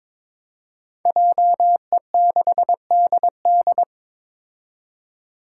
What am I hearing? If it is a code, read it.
Morse code, heard as a keyed tone, about 22 wpm, 700 Hz. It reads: JE6DD
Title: Five Challenging Letter-Number Combinations 22wpm